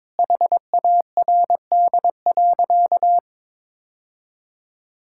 Morse code, heard as a keyed tone, about 22 wpm, 700 Hz. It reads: HARD.
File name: practice / 22wpm / Sentences with 3-Word Phrases - 22wpm